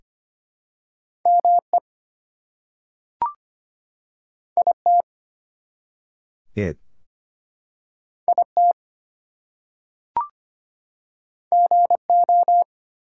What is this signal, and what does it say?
Morse code, heard as a keyed tone, about 25 wpm, 700 Hz. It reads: ME IT IT GO
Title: Two Letter Words - 25wpm